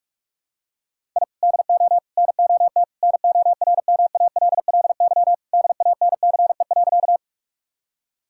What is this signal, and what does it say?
Morse code, heard as a keyed tone, about 46 wpm, 700 Hz. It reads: I DO NOT NORMALLY DANCE.
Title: Sentences from Top 1000 Words - Review Part 6 - 45wpm